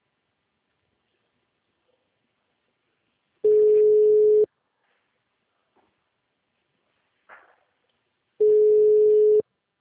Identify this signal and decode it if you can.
Ringback tone (Europe, 425 Hz).
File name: out- -2005- - - .8